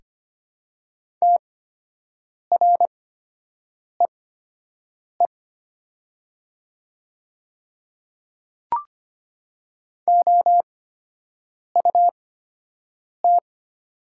Morse code, heard as a keyed tone, about 25 wpm, spaced as for 7 wpm, 690 Hz. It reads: TREE OUT
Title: 8x Character Spacing - Top 500 Words - 25wpm